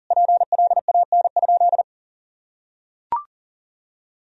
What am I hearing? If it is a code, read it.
Morse code, heard as a keyed tone, about 40 wpm, 700 Hz. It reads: PLAN?
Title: Sentences from Top 1000 Words - Review Part 4 - 40wpm